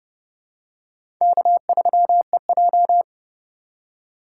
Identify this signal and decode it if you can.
Morse code, heard as a keyed tone, about 31 wpm, 700 Hz. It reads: K3EJ